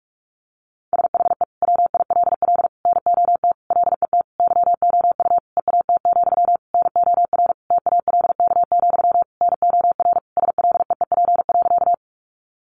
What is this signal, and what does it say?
Morse code, heard as a keyed tone, about 44 wpm, 700 Hz. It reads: SHE WILL NOT LET YOU EAT, NOR TALK, NOR SLEEP.
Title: Sentences from Top 1500 Words 45wpm